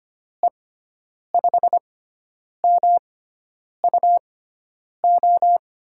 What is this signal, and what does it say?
Morse code, heard as a keyed tone, about 25 wpm, spaced as for 9 wpm, 700 Hz. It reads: E5MUO
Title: USA Call Signs - 6 Characters - 6x Character Spacing - 25wpm